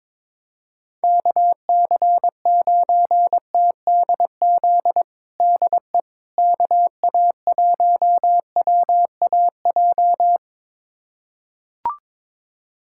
Morse code, heard as a keyed tone, about 22 wpm, 700 Hz. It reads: KC9TDZ DE KA1WAJ
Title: USA Callsign DE Callsign - 6 Characters - 22wpm